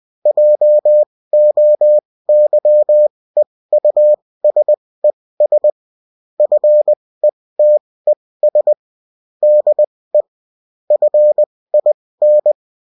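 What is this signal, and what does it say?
Morse code, heard as a keyed tone, about 20 wpm, spaced as for 16 wpm, 600 Hz. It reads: JOYEUSES FETES DE FIN
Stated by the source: UFT bulletin sent in CW.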